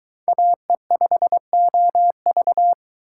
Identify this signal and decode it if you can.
Morse code, heard as a keyed tone, about 23 wpm, 700 Hz. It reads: AE5OV